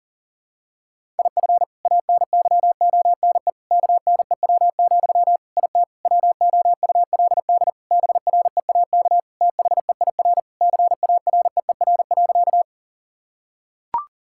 Morse code, heard as a keyed tone, about 37 wpm, 700 Hz. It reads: IF ANYONE KNEW, IT WOULD BREAK THEIR CAREER.
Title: Sentences from Top 1000 Words - Review Part 6 - 40wpm